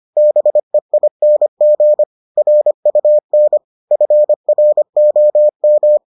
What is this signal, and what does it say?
Morse code, heard as a keyed tone, about 25 wpm, 600 Hz. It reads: BEING RUN FROM